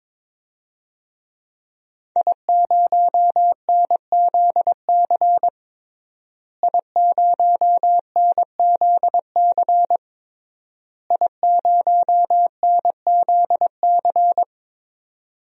Morse code, heard as a keyed tone, about 22 wpm, 700 Hz. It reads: I0NZC I0NZC I0NZC
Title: DXCC-100 Call Signs - 5 Characters - Repeated 3x - 22wpm